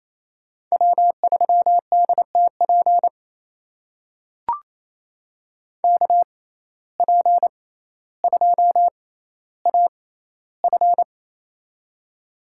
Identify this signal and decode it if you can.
Morse code, heard as a keyed tone, about 28 wpm, spaced as for 14 wpm, 700 Hz. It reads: W3DTP KP2AF
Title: USA Call Signs - 5 Characters - 6x Character Spacing - 28wpm